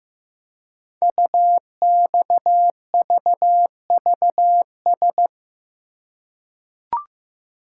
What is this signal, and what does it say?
Morse code, heard as a keyed tone, about 15 wpm, 700 Hz. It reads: UXVVS
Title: Code Groups - 5 Letters - 15wpm